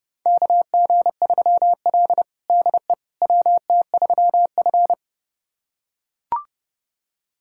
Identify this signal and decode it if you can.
Morse code, heard as a keyed tone, about 30 wpm, 700 Hz. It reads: KG3L DE WT3F
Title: USA Callsign DE Callsign - 4 Characters - 30wpm